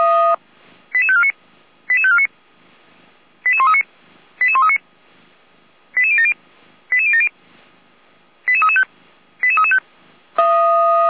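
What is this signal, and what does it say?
Five-tone selective call (ZVEI), heard as quick runs of five tones, tones of about 70 ms. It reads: 88549 88549 88139 88139 88070 88070 88335 88335
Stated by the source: Five-tone ZVEI selective call received on SDR, from audio.f85360.feuerwehr-orlamuende.org, a fire brigade (Feuerwehr Orlamünde) server.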